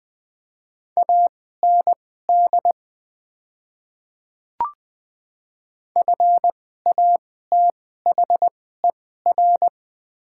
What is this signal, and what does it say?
Morse code, heard as a keyed tone, about 20 wpm, spaced as for 14 wpm, 700 Hz. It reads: AND FATHER
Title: 2x Character Spacing - Top 500 Words - 20wpm